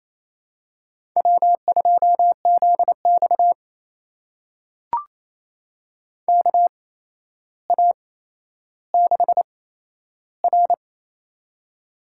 Morse code, heard as a keyed tone, about 28 wpm, spaced as for 12 wpm, 700 Hz. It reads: W2ZX KA6R
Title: USA Call Signs - 4 Characters - 8x Character Spacing - 28wpm